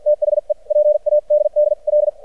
Morse code, heard as a keyed tone, about 56 wpm, spaced as for 45 wpm, 600 Hz. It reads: THE PADDL
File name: archived / CRSnet